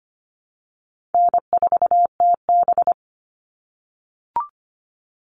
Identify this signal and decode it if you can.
Morse code, heard as a keyed tone, about 25 wpm, 700 Hz. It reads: N4TB
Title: Call Signs - 25wpm